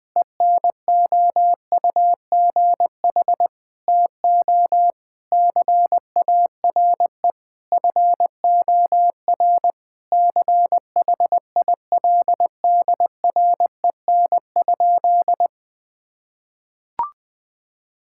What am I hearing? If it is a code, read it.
Morse code, heard as a keyed tone, about 20 wpm, 700 Hz. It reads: ENOUGH TO CARE FOR CHILDREN?
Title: Sentences from Top 1000 Words - Review Part 5 - 20wpm